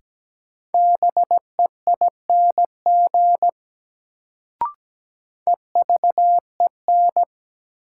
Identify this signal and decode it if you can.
Morse code, heard as a keyed tone, about 17 wpm, 700 Hz. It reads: BEING EVEN
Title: Sets of 4 Words - New Word Drill - 17wpm